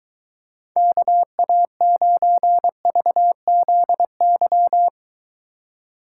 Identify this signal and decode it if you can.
Morse code, heard as a keyed tone, about 23 wpm, 700 Hz. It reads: KA9VZY